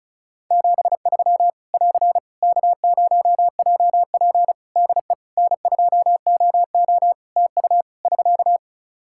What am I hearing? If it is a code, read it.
Morse code, heard as a keyed tone, about 36 wpm, 700 Hz. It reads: 73 <AR> K0JP DE N2OO TU <SK>
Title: Practice QSO - Exchange 4 - Ending - 35wpm